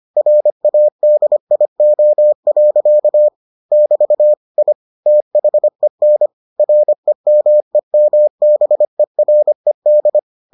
Morse code, heard as a keyed tone, about 25 wpm, 600 Hz. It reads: RADIO. <BT> I THEN REMEMBERED